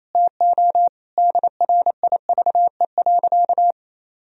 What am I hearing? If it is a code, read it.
Morse code, heard as a keyed tone, about 28 wpm, 700 Hz. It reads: TO DRIVE.
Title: Sentences from Top 1000 Words - Review Part 2 - 28wpm